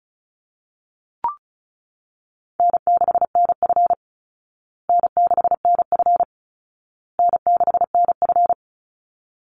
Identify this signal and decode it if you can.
Morse code, heard as a keyed tone, about 34 wpm, 700 Hz. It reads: N6NF N6NF N6NF
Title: Easy Call Signs Repeated 3x - 35wpm